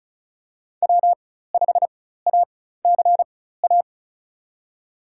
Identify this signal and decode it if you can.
Morse code, heard as a keyed tone, about 34 wpm, spaced as for 16 wpm, 690 Hz. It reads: W5ACA
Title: USA Call Signs - 4x Character Spacing - 35wpm